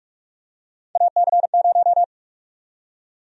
Morse code, heard as a keyed tone, about 45 wpm, 700 Hz. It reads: AC0
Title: USA Call Sign Prefixes - 45wpm